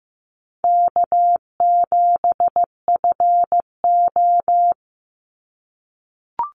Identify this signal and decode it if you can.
Morse code, heard as a keyed tone, about 15 wpm, 700 Hz. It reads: K7FO
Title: NA SOTA Chaser Callsigns 15wpm